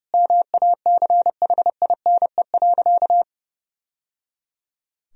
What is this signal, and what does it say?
Morse code, heard as a keyed tone, about 30 wpm, 700 Hz. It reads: MACHINE.
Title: Sentences from Top 1000 Words - Review Part 3 - 30wpm